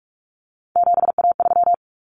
Morse code, heard as a keyed tone, about 44 wpm, 700 Hz. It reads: 7A3
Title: Common Cut Numbers - 3-Digit 45wpm